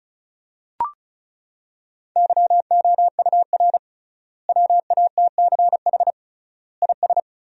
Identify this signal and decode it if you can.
Morse code, heard as a keyed tone, about 34 wpm, 700 Hz. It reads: YOUR WATCH IS
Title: 3x Word-Spacing - Sentences from Top 400 Words - 35wpm